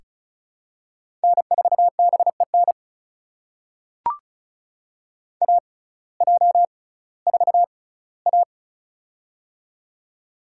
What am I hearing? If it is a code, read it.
Morse code, heard as a keyed tone, about 36 wpm, spaced as for 19 wpm, 700 Hz. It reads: N4BEN AJ4A
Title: USA Call Signs - 6x Character Spacing - 35wpm